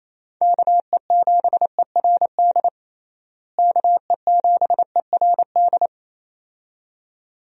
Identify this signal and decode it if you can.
Morse code, heard as a keyed tone, about 27 wpm, 700 Hz. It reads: KE7ERD KE7ERD
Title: USA Call Signs - 6 Characters - Repeated 2x - 28wpm